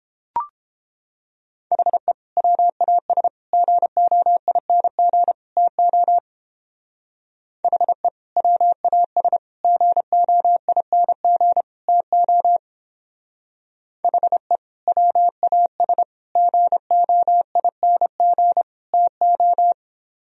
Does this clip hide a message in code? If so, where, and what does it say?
Yes: Morse code from 1.7 to 19.7 s, heard as a keyed tone, about 29 wpm, 700 Hz. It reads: HE WAS GOING TO HE WAS GOING TO HE WAS GOING TO